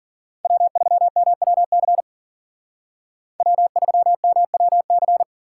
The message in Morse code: W3MWC W3MWC

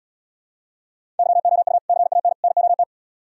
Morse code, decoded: YOU CAN NOT

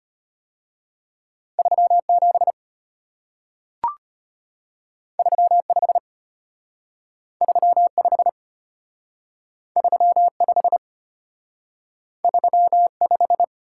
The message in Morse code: 3 7 3 5 3 5 3 5 3 5

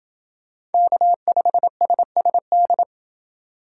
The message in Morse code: K5SSD

